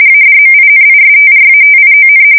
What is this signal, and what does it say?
RTTY, about 45.45 baud. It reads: ADMINISTRATION
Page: 170 Shift 2125/2295